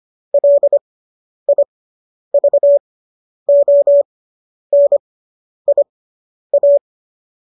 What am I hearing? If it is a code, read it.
Morse code, heard as a keyed tone, about 25 wpm, spaced as for 10 wpm, 570 Hz. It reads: LIVONIA